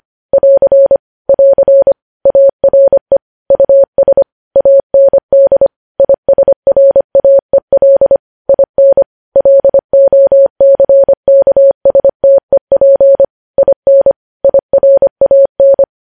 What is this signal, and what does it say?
Morse code, heard as a keyed tone, about 25 wpm, 550 Hz. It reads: <AR> <AR> ARE US AND ISRAEL IN LOCKSTEP IN IRAN
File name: podcast / News headlines-1x-25x25